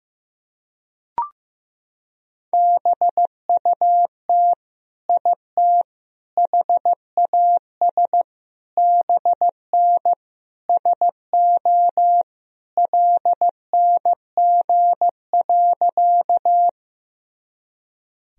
Morse code, heard as a keyed tone, about 16 wpm, 700 Hz. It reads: BUT IT HAS BN SO LNG.
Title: Sentences from Top 100 Words and Abbreviations 15wpm